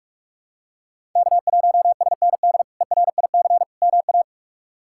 Morse code, heard as a keyed tone, about 46 wpm, 700 Hz. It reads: K1SND ERIC MA
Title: K1USN SST QSO Practice - 45wpm